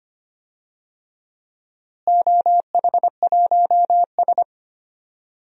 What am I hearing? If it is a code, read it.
Morse code, heard as a keyed tone, about 25 wpm, 700 Hz. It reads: OH1S
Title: DXCC-100 Call Signs - 4 Characters - Repeated 3x - 25wpm